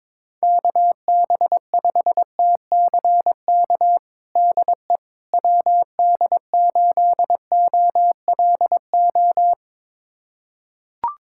Morse code, heard as a keyed tone, about 22 wpm, 700 Hz. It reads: KB5TCK DE WD8OLO
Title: USA Callsign DE Callsign - 6 Characters - 22wpm